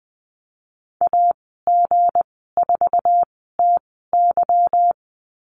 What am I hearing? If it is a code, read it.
Morse code, heard as a keyed tone, about 20 wpm, spaced as for 14 wpm, 700 Hz. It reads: AG4TY
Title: 179 Lesson 29g - Slash - Call Signs 20wpm